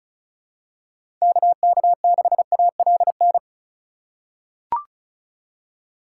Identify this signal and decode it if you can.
Morse code, heard as a keyed tone, about 36 wpm, 700 Hz. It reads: KK6ALN
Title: USA Call Signs - 6 Characters - 4x Character Spacing - 35wpm